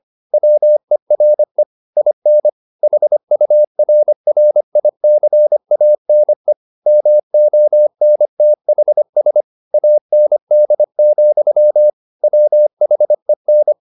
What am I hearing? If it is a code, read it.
Morse code, heard as a keyed tone, about 25 wpm, 600 Hz. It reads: WERE IN HURRICANE MONTHS AND, WHEN
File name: the old man and the sea part 1 25wpm